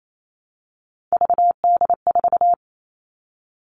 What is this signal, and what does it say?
Morse code, heard as a keyed tone, about 28 wpm, 700 Hz. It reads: VD4
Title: Three Challenging Letter-Number Combinations 28wpm